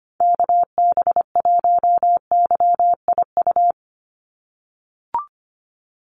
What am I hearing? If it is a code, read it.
Morse code, heard as a keyed tone, about 25 wpm, 700 Hz. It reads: KB1YIU
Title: USA Call Signs 25wpm